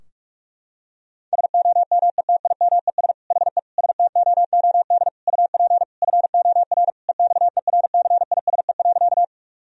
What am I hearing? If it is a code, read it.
Morse code, heard as a keyed tone, about 41 wpm, 700 Hz. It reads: SOMETIMES HE STOOD UP FOR EXERCISE.
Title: Sentences from Top 1000 Words - Review Part 4 - 45wpm